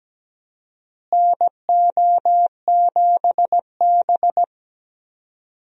Morse code, heard as a keyed tone, about 17 wpm, 700 Hz. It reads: NO7B
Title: USA Call Signs - 4 Characters - Repeated 2x - 17wpm